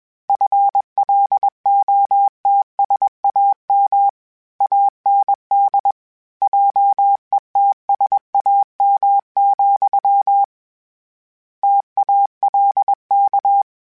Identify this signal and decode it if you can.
Morse code, heard as a keyed tone, about 21 wpm, 800 Hz. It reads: FLOTSAM AND JETSAM, TALK
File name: Misc / Misc-mp3s / Word Pairs @20